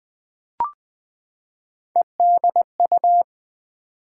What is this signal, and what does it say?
Morse code, heard as a keyed tone, about 20 wpm, 700 Hz. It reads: EDU
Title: Build A Word - 3 Letters At A Time - 9-Letter Words - 20wpm